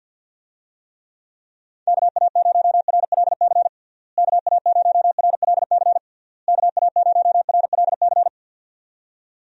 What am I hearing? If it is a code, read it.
Morse code, heard as a keyed tone, about 48 wpm, 700 Hz. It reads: KA0RLC KA0RLC KA0RLC
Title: USA Call Signs - 6 Characters - Repeated 3x - 50wpm